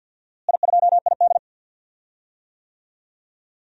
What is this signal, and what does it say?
Morse code, heard as a keyed tone, about 52 wpm, 700 Hz. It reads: I2ID
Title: DXCC-100 Call Signs - 4 Characters - Repeated 3x - 50wpm